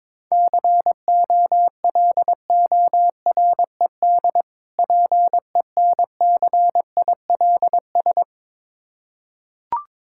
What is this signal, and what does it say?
Morse code, heard as a keyed tone, about 21 wpm, 690 Hz. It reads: COLORED PENCILS
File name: Difficult Words with Hints - 22wpm